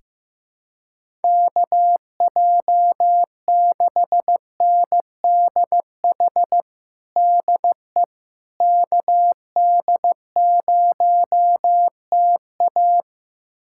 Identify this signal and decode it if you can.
Morse code, heard as a keyed tone, about 15 wpm, 700 Hz. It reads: KJ6NDH DE KD0TA
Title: USA Callsign DE Callsign 15wpm